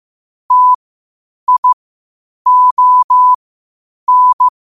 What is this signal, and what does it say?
Morse code, heard as a keyed tone, about 15 wpm, spaced as for 8 wpm, 1000 Hz. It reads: TION
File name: qotdfiles / 8wpm